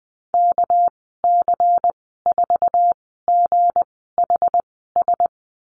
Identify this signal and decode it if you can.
Morse code, heard as a keyed tone, about 19 wpm, spaced as for 14 wpm, 700 Hz. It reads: KC4GHS
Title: 213 Lesson 34g - Number 8 - Call Signs 20wpm